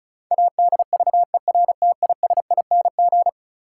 Morse code, heard as a keyed tone, about 34 wpm, 700 Hz. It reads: ADVERTISING
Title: Sentences from Top 1000 Words - Review Part 6 - New Word Drill - 35wpm